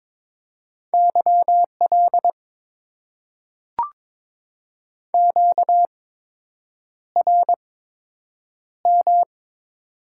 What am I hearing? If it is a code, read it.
Morse code, heard as a keyed tone, about 22 wpm, spaced as for 8 wpm, 700 Hz. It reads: YL QRM
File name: Top 90 QSO Elements - 8x Character Spacing - 22wpm